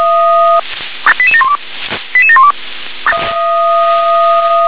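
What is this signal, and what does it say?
Five-tone selective call (ZVEI), heard as quick runs of five tones, tones of about 70 ms. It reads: 88712 88712
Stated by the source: Five-tone ZVEI selective call received by SDR.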